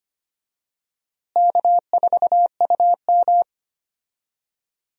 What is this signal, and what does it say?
Morse code, heard as a keyed tone, about 25 wpm, 700 Hz. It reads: K4UM
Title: USA Call Signs - 4 Characters - Repeated 2x - 25wpm